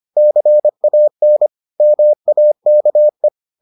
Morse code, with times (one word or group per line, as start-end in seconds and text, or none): 0.2-1.5 CAN
1.8-3.3 MAKE